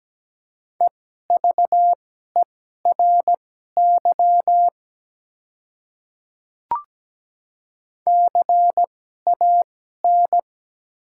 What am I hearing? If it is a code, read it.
Morse code, heard as a keyed tone, about 17 wpm, spaced as for 12 wpm, 700 Hz. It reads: EVERY CAN
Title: 2x Character Spacing - Top 200 Words - 17wpm